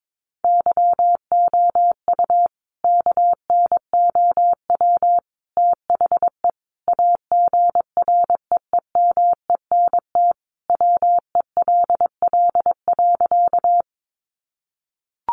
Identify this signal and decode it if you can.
Morse code, heard as a keyed tone, about 22 wpm, 700 Hz. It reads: YOU KNOW THE AGREEMENT WELL.
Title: Sentences from Top 1500 Words 22wpm